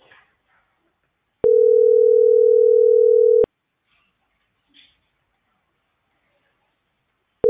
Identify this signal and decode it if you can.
Ringback tone (North America), 440 + 480 Hz.